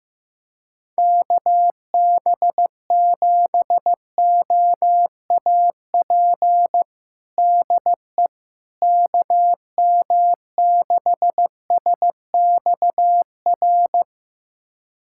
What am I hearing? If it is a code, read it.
Morse code, heard as a keyed tone, about 15 wpm, 700 Hz. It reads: KB7OAP DE KM6SXR
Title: USA Callsign DE Callsign - 6 Characters - 15wpm